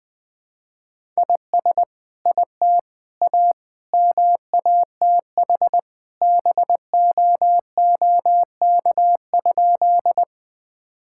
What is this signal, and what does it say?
Morse code, heard as a keyed tone, about 21 wpm, 700 Hz. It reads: IS IT A MATH BOOK?
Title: Sentences from Top 1000 Words - Review Part 6 - 20wpm